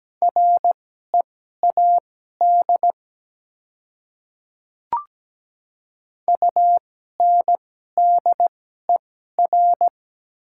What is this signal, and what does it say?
Morse code, heard as a keyed tone, about 17 wpm, spaced as for 12 wpm, 700 Hz. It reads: READ UNDER
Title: 2x Character Spacing - Top 200 Words - 17wpm